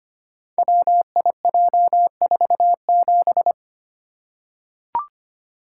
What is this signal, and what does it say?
Morse code, heard as a keyed tone, about 25 wpm, 700 Hz. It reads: WIJ47
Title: Code Groups - 5 Letter:Numbers - 25wpm